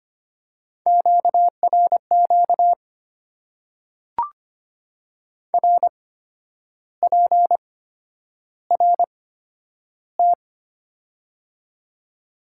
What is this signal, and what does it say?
Morse code, heard as a keyed tone, about 25 wpm, spaced as for 10 wpm, 690 Hz. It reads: QRQ RPRT
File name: Top 90 QSO Elements - 8x Character Spacing - 25wpm